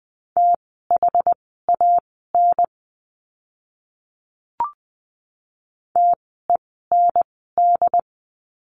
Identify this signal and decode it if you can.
Morse code, heard as a keyed tone, about 20 wpm, spaced as for 14 wpm, 700 Hz. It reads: THAN TEND
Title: 068 Lesson 12f - Letter d - Words 20wpm